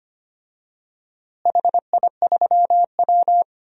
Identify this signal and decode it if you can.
Morse code, heard as a keyed tone, about 25 wpm, 700 Hz. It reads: HI3W
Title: DXCC-100 Call Signs - 4 Characters - Repeated 2x - 25wpm